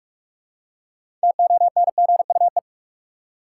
Morse code, heard as a keyed tone, about 41 wpm, 700 Hz. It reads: TONGUE
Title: Six Letter Words - 45wpm